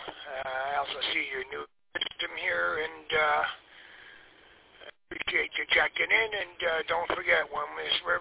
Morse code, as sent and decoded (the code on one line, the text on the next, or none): none